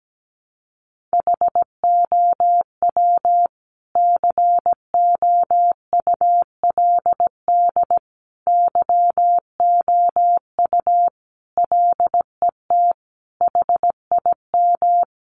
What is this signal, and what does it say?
Morse code, heard as a keyed tone, about 17 wpm, 700 Hz. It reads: HOW COULD YOU LET HIM
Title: Sentences from Top 300 Words 17wpm